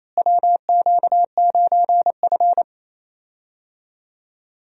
Morse code, heard as a keyed tone, about 28 wpm, 700 Hz. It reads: WQ9F
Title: USA Call Signs - 4 Characters - Repeated 2x - 28wpm